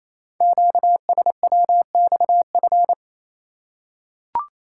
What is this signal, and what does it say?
Morse code, heard as a keyed tone, about 29 wpm, 700 Hz. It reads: QSWXF
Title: Code Groups - 5 Letters - 28wpm